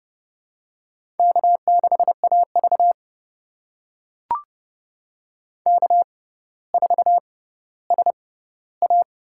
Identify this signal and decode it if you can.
Morse code, heard as a keyed tone, about 30 wpm, spaced as for 15 wpm, 700 Hz. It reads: K6AV K4SA
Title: USA Call Signs - 6x Character Spacing - 30wpm